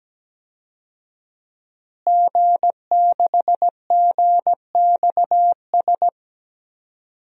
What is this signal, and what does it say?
Morse code, heard as a keyed tone, about 17 wpm, 700 Hz. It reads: G6GXS
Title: DXCC-100 Call Signs - 5 Characters - Repeated 2x - 17wpm